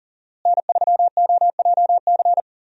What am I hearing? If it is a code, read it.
Morse code, heard as a keyed tone, about 40 wpm, 700 Hz. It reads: N3OJC
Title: USA Call Signs - 5 Characters - Repeated 2x - 40wpm